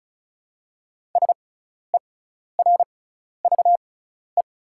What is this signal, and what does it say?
Morse code, heard as a keyed tone, about 36 wpm, spaced as for 12 wpm, 690 Hz. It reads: SERVE
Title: 6x Character Spacing - Top 500 Words - 35wpm